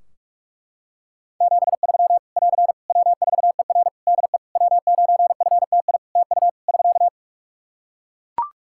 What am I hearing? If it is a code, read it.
Morse code, heard as a keyed tone, about 41 wpm, 700 Hz. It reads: 73 <AR> W4ER DE W9FTI TU <SK>
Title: Practice QSO - Exchange 4 - Ending - 45wpm